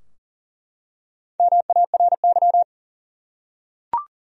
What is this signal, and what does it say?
Morse code, heard as a keyed tone, about 40 wpm, 700 Hz. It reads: MARY
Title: USA Common Names - Max 5 Letters - 40wpm